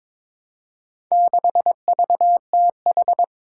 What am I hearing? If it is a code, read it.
Morse code, heard as a keyed tone, about 22 wpm, 700 Hz. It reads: 6VTH
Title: Four Challenging Letter-Number Combinations 22wpm